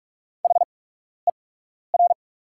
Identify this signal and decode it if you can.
Morse code, heard as a keyed tone, about 44 wpm, spaced as for 12 wpm, 690 Hz. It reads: HER
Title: 8x Character Spacing - Top 300 Words - 45wpm